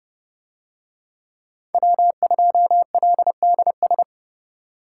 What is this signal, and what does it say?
Morse code, heard as a keyed tone, about 29 wpm, 700 Hz. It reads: W2LDS